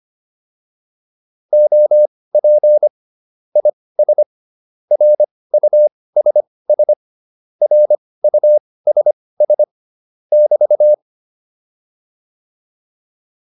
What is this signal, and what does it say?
Morse code, heard as a keyed tone, about 25 wpm, spaced as for 18 wpm, 600 Hz. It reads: OP IS RUSS RUSS <BT>
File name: QSO207 18